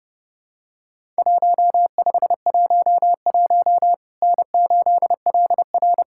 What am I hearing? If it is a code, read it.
Morse code, heard as a keyed tone, about 30 wpm, 700 Hz. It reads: 1511 N8LR
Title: Worked All Europe DX Contest - QTC Exchange - 30wpm